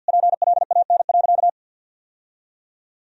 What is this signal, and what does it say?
Morse code, heard as a keyed tone, about 53 wpm, 700 Hz. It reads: PLAN.
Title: Sentences from Top 1000 Words - Review Part 2 - 50wpm